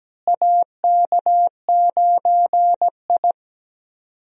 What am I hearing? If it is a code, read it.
Morse code, heard as a keyed tone, about 17 wpm, 700 Hz. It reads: AK9I
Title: USA Call Signs - 4 Characters - Repeated 3x - 17wpm